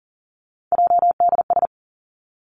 Morse code, heard as a keyed tone, about 42 wpm, 700 Hz. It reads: JDS
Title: Three Challenging Letter-Number Combinations 40wpm